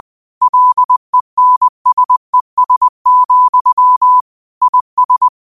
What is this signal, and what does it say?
Morse code, heard as a keyed tone, about 20 wpm, 1000 Hz. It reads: LENSES, IS
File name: qotdfiles / 20wpm